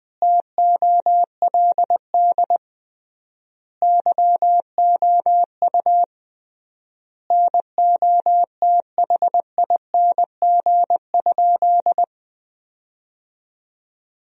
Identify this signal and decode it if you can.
Morse code, heard as a keyed tone, about 21 wpm, 700 Hz. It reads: TOLD YOU NOTHING?
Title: 3x Word-Spacing - Sentences from Top 500 Words - 20wpm